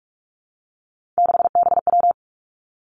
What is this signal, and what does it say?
Morse code, heard as a keyed tone, about 46 wpm, 700 Hz. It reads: 6BW
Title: Three Challenging Letter-Number Combinations 45wpm